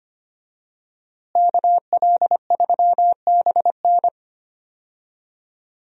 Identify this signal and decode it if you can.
Morse code, heard as a keyed tone, about 25 wpm, 700 Hz. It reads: KL3BN
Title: USA Call Signs - 5 Characters - Repeated 3x - 25wpm